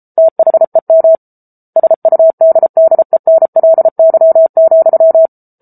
Morse code, heard as a keyed tone, about 34 wpm, 650 Hz. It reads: THEM SUDDENLY,
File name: Book Two / 07-Chapter 7 - 33 WPM